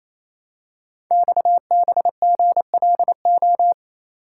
Morse code, heard as a keyed tone, about 27 wpm, 700 Hz. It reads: XBGLO
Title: Code Groups - 5 Letter:Numbers - 28wpm